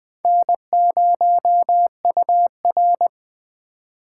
Morse code, heard as a keyed tone, about 23 wpm, 700 Hz. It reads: N0UR
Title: Speed Racing - USA Call Signs 20wpm